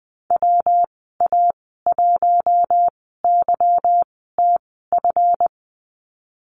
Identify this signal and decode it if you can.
Morse code, heard as a keyed tone, about 19 wpm, spaced as for 14 wpm, 700 Hz. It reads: WA1YTF
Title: 186 Lesson 30g - Letter b - Call Signs 20wpm